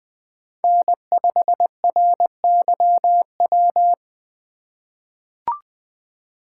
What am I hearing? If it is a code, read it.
Morse code, heard as a keyed tone, about 20 wpm, 700 Hz. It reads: N5RYW
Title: USA Call Signs - 5 Characters - 3x Character Spacing - 20wpm